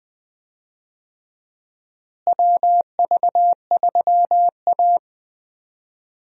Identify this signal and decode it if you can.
Morse code, heard as a keyed tone, about 20 wpm, 700 Hz. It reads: WV3A
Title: USA Call Signs - 4 Characters - Repeated 3x - 20wpm